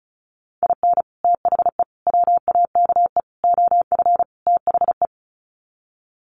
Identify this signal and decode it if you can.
Morse code, heard as a keyed tone, about 34 wpm, 700 Hz. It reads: IN THE WAKE OF THE
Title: Sets of 5 Words 35wpm